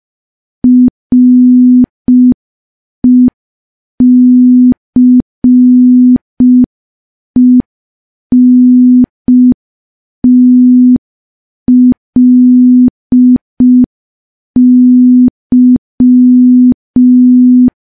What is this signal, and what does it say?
Morse code, heard as a keyed tone, about 5 wpm, 250 Hz. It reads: RECENTLY
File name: morse / morse-training-05WPM